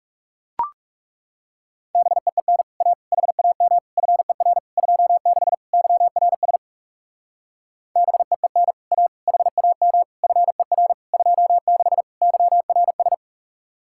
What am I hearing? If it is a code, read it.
Morse code, heard as a keyed tone, about 38 wpm, 700 Hz. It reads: BEEN A HAM FER 26 YRS BEEN A HAM FER 26 YRS